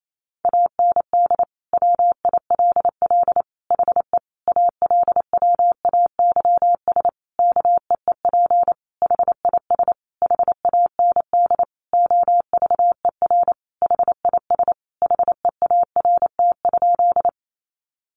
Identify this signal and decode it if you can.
Morse code, heard as a keyed tone, about 28 wpm, 700 Hz. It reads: AND WILL HE ALWAYS KEEP HIS HAND OVER HIS HEART?
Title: Sentences from Top 1200 Words 28wpm